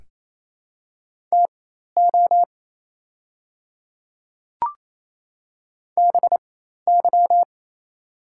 Morse code, heard as a keyed tone, about 27 wpm, spaced as for 13 wpm, 700 Hz. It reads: TO BY
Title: 4x Character Spacing - Top 100 Words - 28wpm